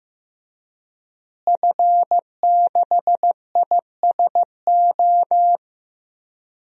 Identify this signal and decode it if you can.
Morse code, heard as a keyed tone, about 15 wpm, 700 Hz. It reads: F6ISO
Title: DXCC-100 Call Signs - 5 Characters - Repeated 3x - 15wpm